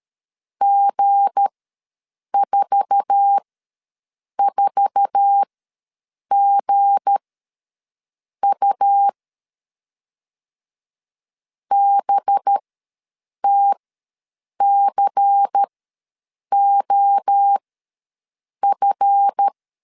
Morse code, heard as a keyed tone, about 13 wpm, spaced as for 7 wpm, 800 Hz. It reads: G44GU BTCOF
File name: LEZIONI / 8 YJ45 / 8 YJ4 ws 12wpm 3249